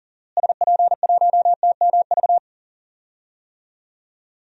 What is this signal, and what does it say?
Morse code, heard as a keyed tone, about 40 wpm, 700 Hz. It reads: SP1TMV
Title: DXCC-100 Call Signs - 6 Characters - Repeated 2x - 40wpm